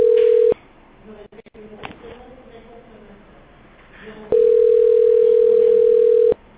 Ringback tone (North America), 440 + 480 Hz.